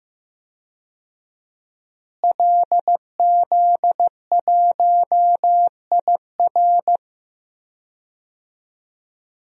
Morse code, heard as a keyed tone, about 15 wpm, 700 Hz. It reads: LZ1IR
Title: DXCC-100 Call Signs - 5 Characters - Repeated 3x - 15wpm